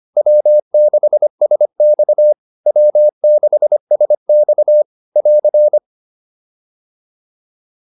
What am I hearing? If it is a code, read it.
Morse code, heard as a keyed tone, about 25 wpm, 600 Hz. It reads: W6SX W6SX <AR>